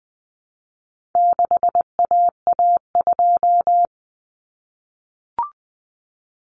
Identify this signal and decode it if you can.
Morse code, heard as a keyed tone, about 20 wpm, 700 Hz. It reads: 6AA2
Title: Common Cut Numbers - 4-Digit 20wpm